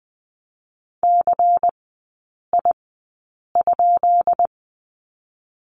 Morse code, heard as a keyed tone, about 20 wpm, spaced as for 8 wpm, 700 Hz. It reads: CI?
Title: 138 Lesson 23e - Letter f - Three Characters 20wpm